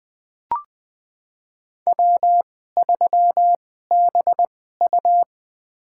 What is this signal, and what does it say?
Morse code, heard as a keyed tone, about 20 wpm, spaced as for 14 wpm, 700 Hz. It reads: W3BU
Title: USA Call Signs - 4 Characters - 2x Character Spacing - 20wpm